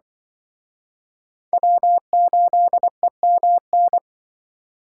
Morse code, heard as a keyed tone, about 24 wpm, 700 Hz. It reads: W8EMN
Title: Speed Racing - USA Call Signs 20wpm